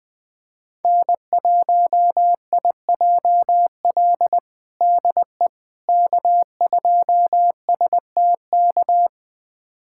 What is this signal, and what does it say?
Morse code, heard as a keyed tone, about 21 wpm, 700 Hz. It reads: N1IJL DE K2STK
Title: USA Callsign DE Callsign - 5 Characters - 20wpm